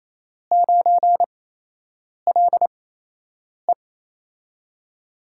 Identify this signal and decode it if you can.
Morse code, heard as a keyed tone, about 28 wpm, spaced as for 8 wpm, 700 Hz. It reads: 9LE
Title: USA Call Signs - 5 Characters - 8x Character Spacing - 28wpm